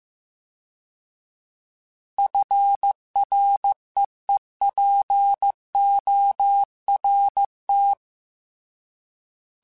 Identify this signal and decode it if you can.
Morse code, heard as a keyed tone, about 15 wpm, 800 Hz. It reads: FREEPORT